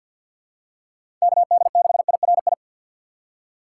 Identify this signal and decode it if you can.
Morse code, heard as a keyed tone, about 52 wpm, 700 Hz. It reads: KD6IRI
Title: K1USN SST QSO Practice - 50wpm